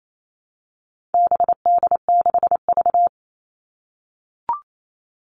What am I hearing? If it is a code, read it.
Morse code, heard as a keyed tone, about 28 wpm, 700 Hz. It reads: BD6V
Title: All Cut Numbers - 4-Digit 28wpm